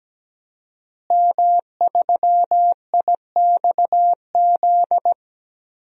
Morse code, heard as a keyed tone, about 17 wpm, 700 Hz. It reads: M3IXZ
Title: DXCC-100 Call Signs - 5 Characters - Repeated 3x - 17wpm